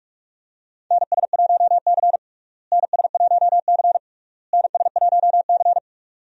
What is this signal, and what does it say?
Morse code, heard as a keyed tone, about 46 wpm, 700 Hz. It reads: NS1C NS1C NS1C
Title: USA Call Signs - 4 Characters - Repeated 3x - 45wpm